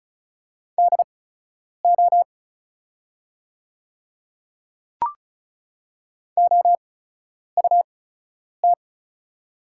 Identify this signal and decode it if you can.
Morse code, heard as a keyed tone, about 36 wpm, spaced as for 10 wpm, 700 Hz. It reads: DO OUT
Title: 8x Character Spacing - Top 100 Words - 35wpm